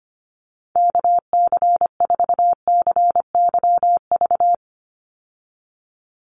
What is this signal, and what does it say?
Morse code, heard as a keyed tone, about 25 wpm, 700 Hz. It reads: KC4CYV
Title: USA Call Signs 25wpm